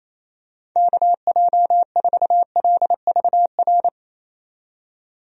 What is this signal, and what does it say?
Morse code, heard as a keyed tone, about 28 wpm, 700 Hz. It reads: KJ4LVR
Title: USA Call Signs - 6 Characters - 28wpm